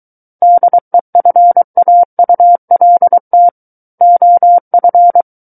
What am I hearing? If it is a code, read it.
Morse code, heard as a keyed tone, about 23 wpm, 700 Hz. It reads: DEFAULT OF
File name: Book One / 14-Chapter 14 - 23 WPM